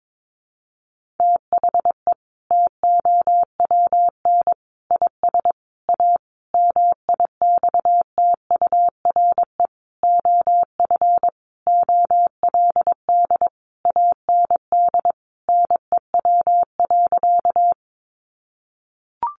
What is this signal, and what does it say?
Morse code, heard as a keyed tone, about 22 wpm, 700 Hz. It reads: THE TOWN IS A MIXTURE OF OLD AND NEW.
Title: Sentences from Top 1500 Words 22wpm